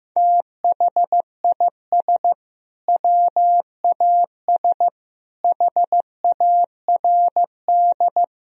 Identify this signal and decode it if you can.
Morse code, heard as a keyed tone, about 15 wpm, 700 Hz. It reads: THIS WAS HARD
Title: Sentences from Top 1000 Words - Review Part 1 - 15wpm